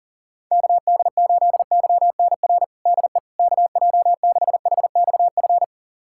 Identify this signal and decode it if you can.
Morse code, heard as a keyed tone, about 37 wpm, 700 Hz. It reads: KD8YNR DE KJ6HXF
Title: USA Callsign DE Callsign 40wpm